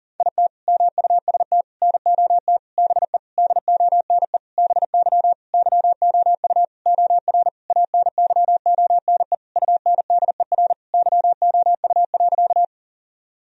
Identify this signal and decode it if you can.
Morse code, heard as a keyed tone, about 40 wpm, 700 Hz. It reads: IT MUST NOT BE DONE BY YOU OR ANYONE UNDER YOU.
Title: Sentences from Top 1000 Words - Review Part 1 - 40wpm